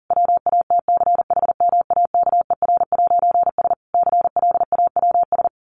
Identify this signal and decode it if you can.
Morse code, heard as a keyed tone, about 37 wpm, 700 Hz. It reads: WATCHMAKER'S CLAWS
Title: A Sound Of Thunder 40wpm